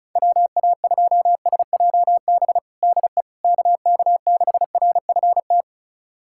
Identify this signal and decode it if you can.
Morse code, heard as a keyed tone, about 36 wpm, 700 Hz. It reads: WA2SJB DE KK6RFT
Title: USA Callsign DE Callsign - 6 Characters - 35wpm